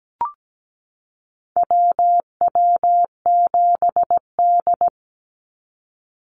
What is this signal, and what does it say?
Morse code, heard as a keyed tone, about 17 wpm, 700 Hz. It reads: WW7D
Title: NA SOTA Activator Callsigns 17wpm